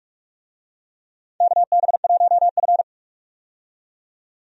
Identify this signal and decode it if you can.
Morse code, heard as a keyed tone, about 47 wpm, 700 Hz. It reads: KB1F
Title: USA Call Signs - 4 Characters - 45wpm